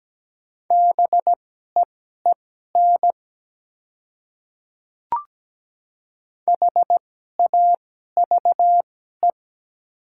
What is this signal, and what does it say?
Morse code, heard as a keyed tone, about 17 wpm, spaced as for 12 wpm, 700 Hz. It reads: BEEN HAVE
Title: 2x Character Spacing - Top 100 Words - 17wpm